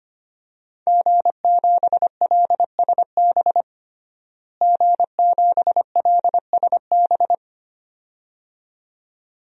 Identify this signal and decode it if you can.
Morse code, heard as a keyed tone, about 25 wpm, 700 Hz. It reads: G7LSB G7LSB
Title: DXCC-100 Call Signs - 5 Characters - Repeated 2x - 25wpm